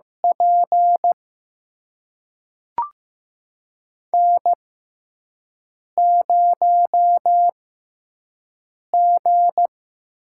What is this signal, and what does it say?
Morse code, heard as a keyed tone, about 15 wpm, spaced as for 5 wpm, 700 Hz. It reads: P N0G
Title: USA Call Signs - 5 Characters - 6x Character Spacing - 15wpm